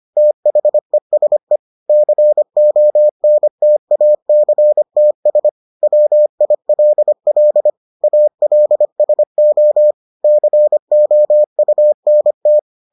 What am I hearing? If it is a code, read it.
Morse code, heard as a keyed tone, about 25 wpm, 600 Hz. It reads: THESE CONTACTS WILL ALSO COUNT